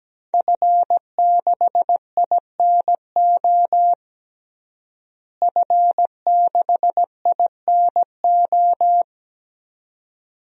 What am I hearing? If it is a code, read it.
Morse code, heard as a keyed tone, about 17 wpm, 700 Hz. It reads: F6INO F6INO
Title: DXCC-100 Call Signs - 5 Characters - Repeated 2x - 17wpm